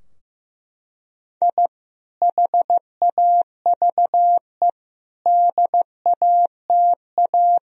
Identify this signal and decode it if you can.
Morse code, heard as a keyed tone, about 15 wpm, 700 Hz. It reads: I HAVE DATA
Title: Sentences from Top 1000 Words - Review Part 3 - 15wpm